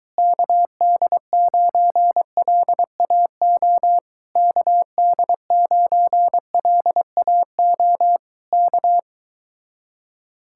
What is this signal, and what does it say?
Morse code, heard as a keyed tone, about 23 wpm, 700 Hz. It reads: KD9LAO KD9LAO K